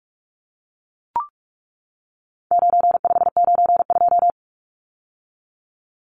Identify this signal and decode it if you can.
Morse code, heard as a keyed tone, about 44 wpm, 700 Hz. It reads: 9592
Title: Numbers - 4-Digit 45wpm